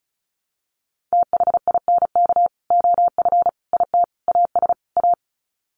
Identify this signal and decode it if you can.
Morse code, heard as a keyed tone, about 34 wpm, 700 Hz. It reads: THINK OF IT AS A
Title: Sets of 5 Words Encore2 35wpm